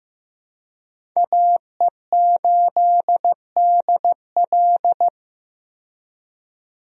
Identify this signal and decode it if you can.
Morse code, heard as a keyed tone, about 15 wpm, 700 Hz. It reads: AE8DL
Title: USA Call Signs - 5 Characters - 15wpm